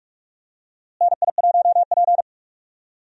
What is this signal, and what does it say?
Morse code, heard as a keyed tone, about 44 wpm, 700 Hz. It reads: NI1P
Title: USA Call Signs - 4 Characters - Repeated 3x - 45wpm